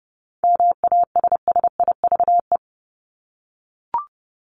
Morse code, heard as a keyed tone, about 30 wpm, 700 Hz. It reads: MASSIVE
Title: Top 1751-2000 Words 30wpm